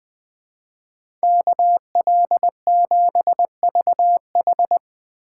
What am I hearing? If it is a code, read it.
Morse code, heard as a keyed tone, about 20 wpm, 700 Hz. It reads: KL7VH